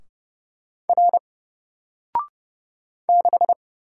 Morse code, heard as a keyed tone, about 30 wpm, 700 Hz. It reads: R 6
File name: Single-Letter-Number-30wpm